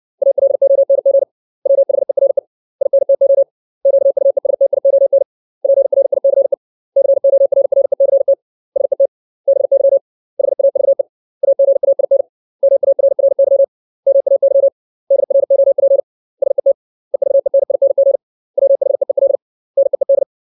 Morse code, heard as a keyed tone, about 60 wpm, 550 Hz. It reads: ALONG WHERE INTO QUESTION PRICE CORRECT SET BY HAVE AGAIN MARRY MAY DROP SET EVENING WHEEL NEED
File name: 632 words / 632 Words 60WPM